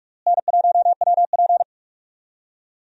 Morse code, heard as a keyed tone, about 46 wpm, 700 Hz. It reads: N1WP
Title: USA Call Signs - 4 Characters - Repeated 3x - 45wpm